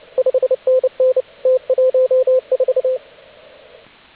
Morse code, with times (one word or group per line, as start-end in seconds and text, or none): 0.2-1.2 5NN
1.5-3.0 T14